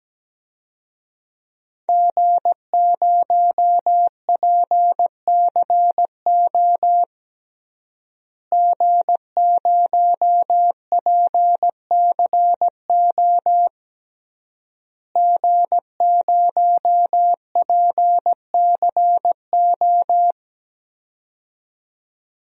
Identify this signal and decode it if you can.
Morse code, heard as a keyed tone, about 17 wpm, 700 Hz. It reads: G0PCO G0PCO G0PCO
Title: DXCC-100 Call Signs - 5 Characters - Repeated 3x - 17wpm